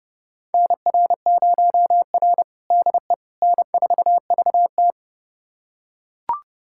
Morse code, heard as a keyed tone, about 30 wpm, 700 Hz. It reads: NR0R DE N4VT
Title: USA Callsign DE Callsign - 4 Characters - 30wpm